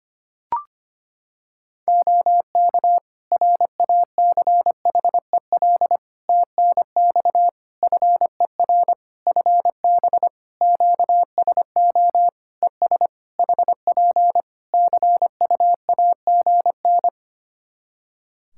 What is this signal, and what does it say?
Morse code, heard as a keyed tone, about 25 wpm, 700 Hz. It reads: OK RACHEL TNX FER FB QSO ES HP CUAGN
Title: K1USN SST QSO Practice 25wpm